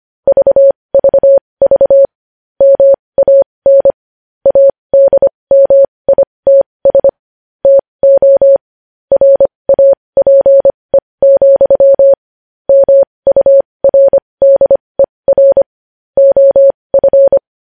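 Morse code, heard as a keyed tone, about 25 wpm, spaced as for 20 wpm, 550 Hz. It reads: VVV MAN ADMITS TO RAPE, MURDER OF
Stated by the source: Morse code news headlines sent at 20 words per minute.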